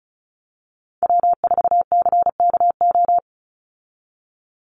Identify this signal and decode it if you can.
Morse code, heard as a keyed tone, about 34 wpm, 700 Hz. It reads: W4CKO
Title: USA Call Signs 35wpm